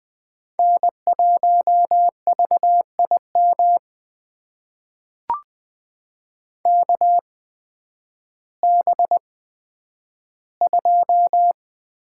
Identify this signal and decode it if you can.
Morse code, heard as a keyed tone, about 20 wpm, spaced as for 11 wpm, 700 Hz. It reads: N1VIM KB2
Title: USA Call Signs - 8x Character Spacing - 20wpm